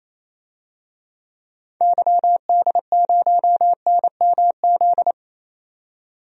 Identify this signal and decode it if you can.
Morse code, heard as a keyed tone, about 28 wpm, 700 Hz. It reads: YD0NMZ
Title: DXCC-100 Call Signs - 6 Characters - 28wpm